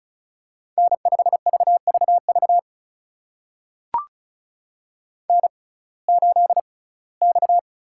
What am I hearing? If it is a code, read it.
Morse code, heard as a keyed tone, about 36 wpm, spaced as for 21 wpm, 700 Hz. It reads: N5VVV N8X
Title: USA Call Signs - 5 Characters - 6x Character Spacing - 35wpm